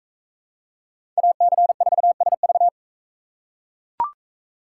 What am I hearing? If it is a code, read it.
Morse code, heard as a keyed tone, about 40 wpm, 700 Hz. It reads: AC4SV